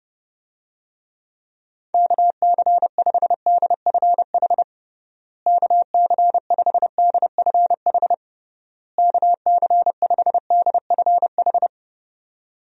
Morse code, heard as a keyed tone, about 32 wpm, 700 Hz. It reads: KC5DFH KC5DFH KC5DFH
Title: USA Call Signs - 6 Characters - Repeated 3x - 30wpm